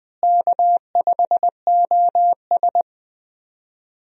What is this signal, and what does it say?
Morse code, heard as a keyed tone, about 20 wpm, 700 Hz. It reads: K5OS
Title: USA Call Signs - 4 Characters - Repeated 3x - 20wpm